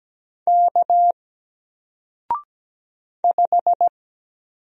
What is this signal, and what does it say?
Morse code, heard as a keyed tone, about 17 wpm, 700 Hz. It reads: K 5
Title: Single Letter-Number 17wpm